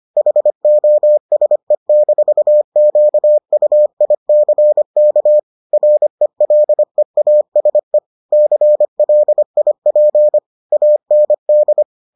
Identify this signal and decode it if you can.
Morse code, heard as a keyed tone, about 25 wpm, 600 Hz. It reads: HOSE-QUICK RELEASE CLIP AND